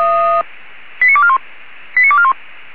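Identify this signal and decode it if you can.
Five-tone selective call (ZVEI), heard as quick runs of five tones, tones of about 70 ms. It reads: 89241 89241
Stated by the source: ZVEI five-tone call 89242, received by SDR.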